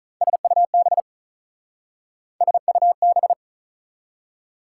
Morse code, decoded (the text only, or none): SUB SUB